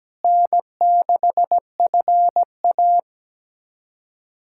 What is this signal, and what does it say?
Morse code, heard as a keyed tone, about 17 wpm, 700 Hz. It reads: N6FA
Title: USA Callsign DE Callsign - 4 Characters - 17wpm